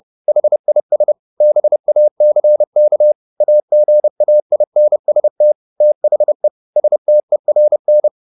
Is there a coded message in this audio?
Yes: Morse code from 0.3 to 8.1 s, heard as a keyed tone, about 30 wpm, 600 Hz. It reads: HIS BACK AGAINST THE STERN